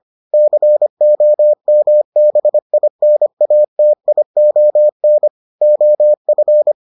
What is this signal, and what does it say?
Morse code, heard as a keyed tone, about 25 wpm, 600 Hz. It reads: COMBINATION OF